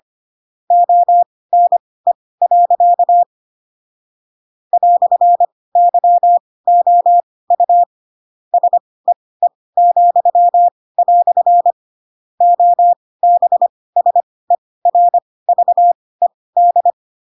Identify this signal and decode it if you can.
Morse code, heard as a keyed tone, about 25 wpm, spaced as for 18 wpm, 700 Hz. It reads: ONE. "YOU SEE," OBSERVED
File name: Pinocchio / Pinocchio ch7 12 25 ik1hge